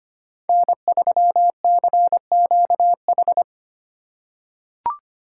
Morse code, heard as a keyed tone, about 25 wpm, 700 Hz. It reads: N3CQH
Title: USA Call Signs - 5 Characters - 4x Character Spacing - 25wpm